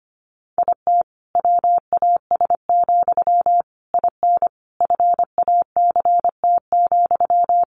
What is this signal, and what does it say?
Morse code, heard as a keyed tone, about 25 wpm, 700 Hz. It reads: IT WAS, IN FACT,
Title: Sentences from Top 1400 Words 25wpm